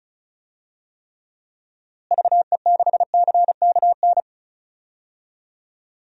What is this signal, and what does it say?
Morse code, heard as a keyed tone, about 34 wpm, 700 Hz. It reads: VE6CKN
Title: DXCC-100 Call Signs - 6 Characters - Repeated 3x - 35wpm